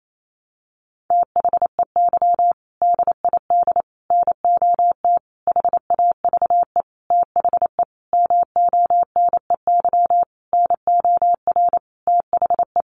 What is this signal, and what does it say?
Morse code, heard as a keyed tone, about 28 wpm, 700 Hz. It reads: THEY DID NOT HAVE THE MONEY NOR THE
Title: Sentences from Top 1500 Words 28wpm